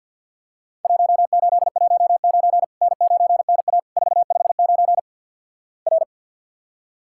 Morse code, heard as a keyed tone, about 50 wpm, 700 Hz. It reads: 1819 N9NA 459 R
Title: Worked All Europe DX Contest - QTC Exchange - 50wpm